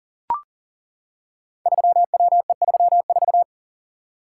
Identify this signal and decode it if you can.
Morse code, heard as a keyed tone, about 41 wpm, 700 Hz. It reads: 3WE34
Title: Code Groups - 5 Letter:Numbers - 40wpm